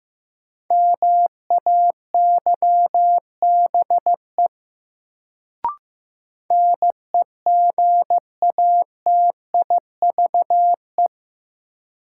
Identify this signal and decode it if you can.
Morse code, heard as a keyed tone, about 15 wpm, 700 Hz. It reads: MAYBE NEGATIVE
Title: Sentences from Top 1000 Words - New Word Drill - 15wpm